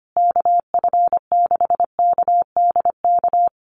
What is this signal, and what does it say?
Morse code, heard as a keyed tone, about 25 wpm, 700 Hz. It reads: KF6KDK
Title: USA Call Signs Repeated 3x 25wpm